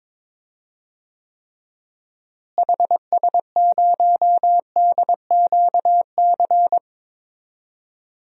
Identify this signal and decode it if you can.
Morse code, heard as a keyed tone, about 21 wpm, 700 Hz. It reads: HS0DQC
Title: DXCC-100 Call Signs - 6 Characters - Repeated 2x - 22wpm